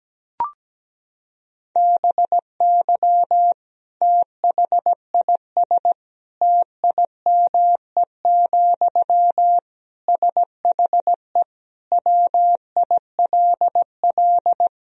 Morse code, heard as a keyed tone, about 17 wpm, 700 Hz. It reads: BY THIS TIME, SHE WILL
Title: Sentences from Top 1000 Words - Review Part 4 - 17wpm